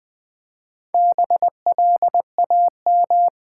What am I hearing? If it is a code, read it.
Morse code, heard as a keyed tone, about 20 wpm, 700 Hz. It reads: BLAM
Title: Build A Word - 1 Letter At A Time - 5-Letter Words - 20wpm